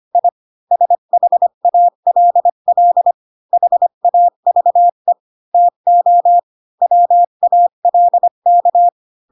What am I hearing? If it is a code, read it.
Morse code, heard as a keyed tone, about 25 wpm, 700 Hz. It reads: I SHALL HAVE TO WALK
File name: Pinocchio / Pinocchio ch7 20 25 ik1hge